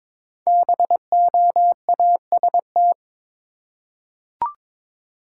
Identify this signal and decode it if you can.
Morse code, heard as a keyed tone, about 22 wpm, 700 Hz. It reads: BOAST
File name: 5 Letter Words - First 3 Letters Given - 22wpm